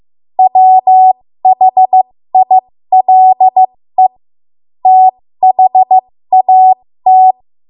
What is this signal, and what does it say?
Morse code, heard as a keyed tone, about 15 wpm, spaced as for 13 wpm, 750 Hz. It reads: WHILE THAT
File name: Morse / 13WPM